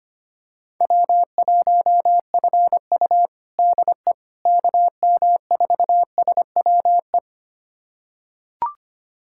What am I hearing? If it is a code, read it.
Morse code, heard as a keyed tone, about 25 wpm, 700 Hz. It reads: W1FU DE KM4SWE
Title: USA Callsign DE Callsign 25wpm